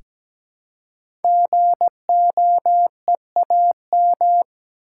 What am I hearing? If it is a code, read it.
Morse code, heard as a keyed tone, about 17 wpm, 700 Hz. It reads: GOEAM
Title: Code Groups - 5 Letters - 17wpm